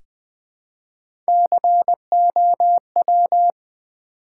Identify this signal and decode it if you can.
Morse code, heard as a keyed tone, about 20 wpm, 700 Hz. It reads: COW
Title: Easy Words with Hints - 20wpm